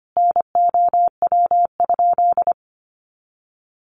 Morse code, heard as a keyed tone, about 25 wpm, 700 Hz. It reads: NOW?
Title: Sentences from Top 200 Words 25wpm